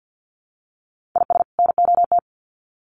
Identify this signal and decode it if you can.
Morse code, heard as a keyed tone, about 50 wpm, 700 Hz. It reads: IS NOT
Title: Sets of 2 Words 50wpm